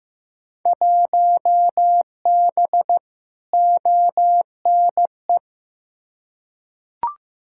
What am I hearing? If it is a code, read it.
Morse code, heard as a keyed tone, about 15 wpm, 700 Hz. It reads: 1B ONE
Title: Field Day Exchanges - 15wpm